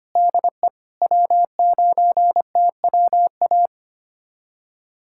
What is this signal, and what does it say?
Morse code, heard as a keyed tone, about 25 wpm, 700 Hz. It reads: DE W9TWA
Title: USA Callsign DE Callsign - 5 Characters - 25wpm